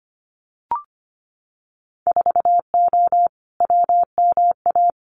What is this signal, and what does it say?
Morse code, heard as a keyed tone, about 25 wpm, 700 Hz. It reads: 4O WMA
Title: Winter Field Day Exchanges Encore 25wpm